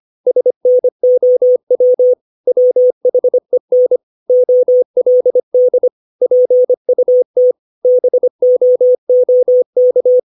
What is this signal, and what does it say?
Morse code, heard as a keyed tone, about 25 wpm, 500 Hz. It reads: SNOW WHEN OLD PUT BOOK
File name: 632 words / 632 Words 25WPM